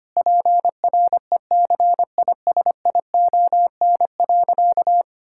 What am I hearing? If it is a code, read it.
Morse code, heard as a keyed tone, about 25 wpm, 700 Hz. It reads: PRECISION.
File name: practice / 25wpm / Difficult Sentences with Hints - 25wpm